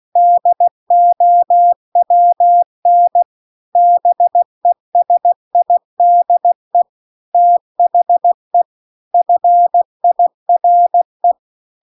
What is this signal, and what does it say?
Morse code, heard as a keyed tone, about 16 wpm, 700 Hz. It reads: DOWN BESIDE THE FIRE